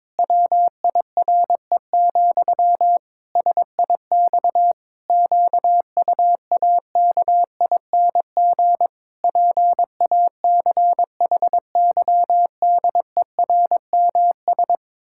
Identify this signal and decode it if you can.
Morse code, heard as a keyed tone, about 22 wpm, 700 Hz. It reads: WIRE, SIX QUAKING PACHYDERMS
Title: Pangram Challenge v2 - 22wpm